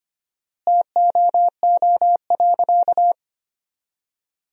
Morse code, heard as a keyed tone, about 25 wpm, 700 Hz. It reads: TOO.
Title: 4x Word-Spacing - Sentences from Top 200 Words - 25wpm